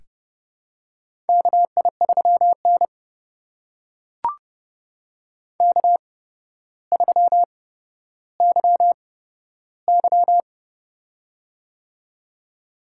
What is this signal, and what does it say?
Morse code, heard as a keyed tone, about 28 wpm, spaced as for 13 wpm, 700 Hz. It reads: KI3N K3YY
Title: USA Call Signs - 4 Characters - 8x Character Spacing - 30wpm